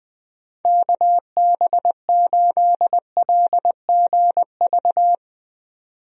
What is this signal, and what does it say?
Morse code, heard as a keyed tone, about 20 wpm, 700 Hz. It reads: KB8LGV
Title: USA Call Signs - 6 Characters - 20wpm